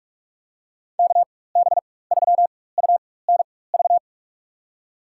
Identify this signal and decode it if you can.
Morse code, heard as a keyed tone, about 44 wpm, spaced as for 21 wpm, 700 Hz. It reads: KB3UNV
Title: USA Call Signs - 6 Characters - 4x Character Spacing - 45wpm